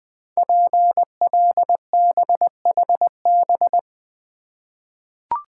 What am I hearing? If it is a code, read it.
Morse code, heard as a keyed tone, about 20 wpm, 700 Hz. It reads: PLBHB
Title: Code Groups - 5 Letter:Numbers - 20wpm